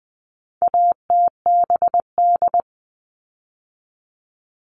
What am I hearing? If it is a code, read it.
Morse code, heard as a keyed tone, about 19 wpm, 700 Hz. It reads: ATBD
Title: All Cut Numbers - 4-Digit 20wpm